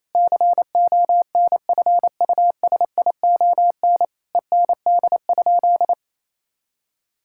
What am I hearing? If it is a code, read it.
Morse code, heard as a keyed tone, about 27 wpm, 700 Hz. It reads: CONFUSION END?
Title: Sentences from Top 1600 Words 28wpm